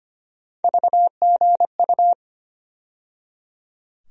Morse code, heard as a keyed tone, about 25 wpm, 700 Hz. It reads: VGU
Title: USA Call Sign Postfixes - 3 Letters 25wpm